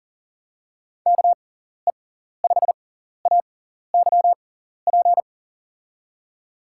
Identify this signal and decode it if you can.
Morse code, heard as a keyed tone, about 37 wpm, spaced as for 14 wpm, 700 Hz. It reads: KE5AYP
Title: USA Call Signs - 6x Character Spacing - 40wpm